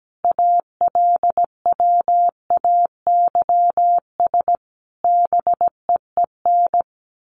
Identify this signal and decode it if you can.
Morse code, heard as a keyed tone, about 17 wpm, 700 Hz. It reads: ALWAYS BEEN
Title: Sentences from Top 1400 Words 17wpm